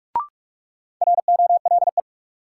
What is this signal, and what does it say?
Morse code, heard as a keyed tone, about 46 wpm, 700 Hz. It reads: ROLE
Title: Sentences from Top 1000 Words - Review Part 6 - New Word Drill - 45wpm